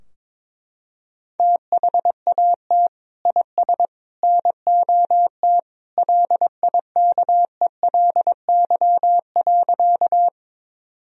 Morse code, heard as a keyed tone, about 21 wpm, 700 Hz. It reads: THAT IS NOT LIKELY.
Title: Sentences from Top 1000 Words - Review Part 4 - 22wpm